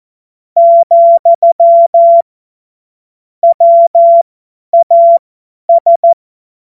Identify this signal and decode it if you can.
Morse code, heard as a keyed tone, about 14 wpm, spaced as for 10 wpm, 680 Hz. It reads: , WAS
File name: Morsekurs / Witze / Witz 11. Januar 2022, 50 F70